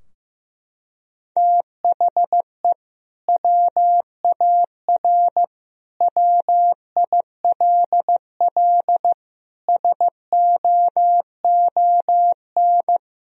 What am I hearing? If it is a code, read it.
Morse code, heard as a keyed tone, about 16 wpm, 700 Hz. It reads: THE WAR WILL SOON
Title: Sentences from Top 1000 Words - Review Part 2 - 15wpm